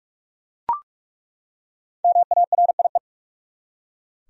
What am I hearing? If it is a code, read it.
Morse code, heard as a keyed tone, about 47 wpm, 700 Hz. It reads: MARIE
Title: USA Common Names - 45wpm